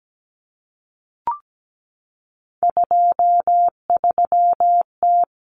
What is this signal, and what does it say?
Morse code, heard as a keyed tone, about 17 wpm, 700 Hz. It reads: 23T
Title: Common Cut Numbers - 3-Digit 17wpm